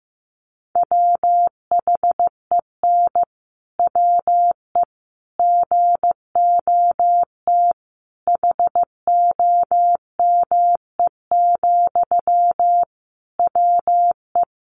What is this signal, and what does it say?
Morse code, heard as a keyed tone, about 15 wpm, 700 Hz. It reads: WHEN WE GOT HOME, WE
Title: Sentences from Top 1400 Words 15wpm